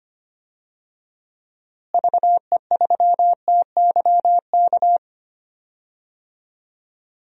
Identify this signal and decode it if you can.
Morse code, heard as a keyed tone, about 25 wpm, 700 Hz. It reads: VE3TYK
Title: DXCC-100 Call Signs - 6 Characters - Repeated 3x - 25wpm